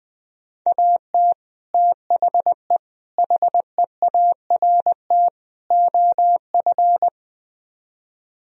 Morse code, heard as a keyed tone, about 20 wpm, 700 Hz. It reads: AT THE HEART OF
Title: Speed-Racing - Sets of 4 Words 20wpm